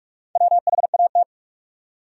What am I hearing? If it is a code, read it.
Morse code, heard as a keyed tone, about 46 wpm, 700 Hz. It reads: WHAT
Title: Top 100 Words Repeated 45wpm